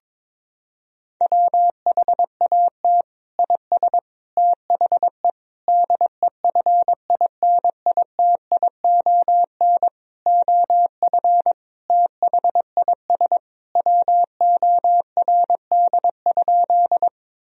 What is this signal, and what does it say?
Morse code, heard as a keyed tone, about 22 wpm, 700 Hz. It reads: WHAT IS THE DEFINITION OF THIS WORD?
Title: Sentences from Top 1000 Words - Review Part 4 - 22wpm